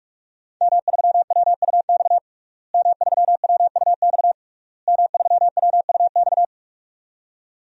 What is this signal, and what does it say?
Morse code, heard as a keyed tone, about 41 wpm, 700 Hz. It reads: M3WUX M3WUX M3WUX
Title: DXCC-100 Call Signs - 5 Characters - Repeated 3x - 45wpm